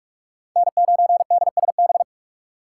Morse code, heard as a keyed tone, about 44 wpm, 690 Hz. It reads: N9DSB
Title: USA Call Signs - 5 Characters - 4x Character Spacing - 45wpm